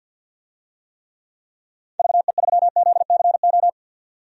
Morse code, heard as a keyed tone, about 52 wpm, 700 Hz. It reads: VE3ZCO
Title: DXCC-100 Call Signs - 6 Characters - 50wpm